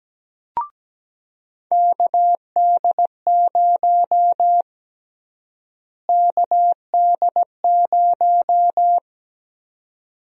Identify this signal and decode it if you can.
Morse code, heard as a keyed tone, about 17 wpm, 700 Hz. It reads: KD0 KD0
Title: USA Call Sign Prefixes - Repeated 2x - 17wpm